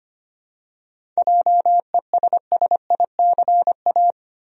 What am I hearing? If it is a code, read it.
Morse code, heard as a keyed tone, about 25 wpm, 700 Hz. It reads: JESSICA
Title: USA Common Names - 25wpm